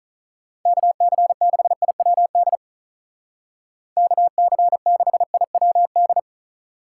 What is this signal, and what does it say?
Morse code, heard as a keyed tone, about 45 wpm, 700 Hz. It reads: KC6IWD KC6IWD